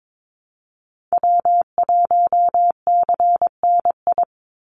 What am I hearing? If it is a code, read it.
Morse code, heard as a keyed tone, about 22 wpm, 700 Hz. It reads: W1CNI
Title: USA Call Signs 22wpm